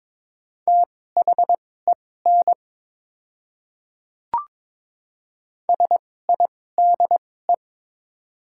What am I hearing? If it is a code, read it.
Morse code, heard as a keyed tone, about 22 wpm, spaced as for 16 wpm, 690 Hz. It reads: THEN SIDE
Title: 2x Character Spacing - Top 100 Words - 22wpm